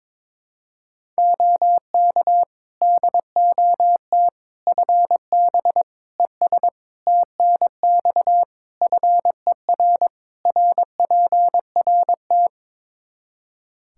Morse code, heard as a keyed tone, about 22 wpm, 700 Hz. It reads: OK DOT FB ES TNX FER RPRT
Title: Practice QSO - Exchange 2 - Info - 22wpm